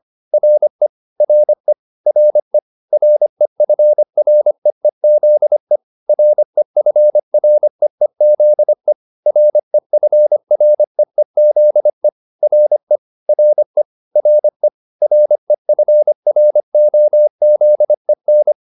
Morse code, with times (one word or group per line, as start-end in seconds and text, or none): 0.3-0.9 RE
1.2-1.7 RE
2.1-2.6 RE
2.9-5.8 REFREEZE
6.1-8.9 REFREEZE
9.3-12.1 REFREEZE
12.4-13.0 RE
13.3-13.8 RE
14.2-14.7 RE
15.0-18.5 REFROZEN